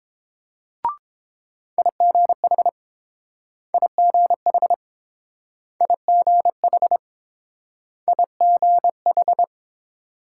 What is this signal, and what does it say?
Morse code, heard as a keyed tone, about 29 wpm, 700 Hz. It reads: IGH IGH IGH IGH